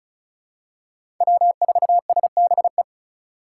Morse code, heard as a keyed tone, about 34 wpm, 700 Hz. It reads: W4SBE
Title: USA Call Signs - 5 Characters - 2x Character Spacing - 35wpm